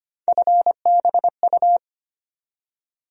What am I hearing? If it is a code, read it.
Morse code, heard as a keyed tone, about 25 wpm, 700 Hz. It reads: FBU
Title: USA Call Sign Postfixes - 3 Letters 25wpm